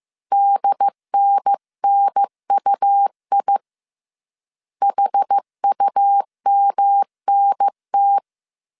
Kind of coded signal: Morse code